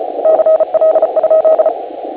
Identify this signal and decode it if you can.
Morse code, heard as a keyed tone, about 33 wpm, 630 Hz, with radio noise. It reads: CL?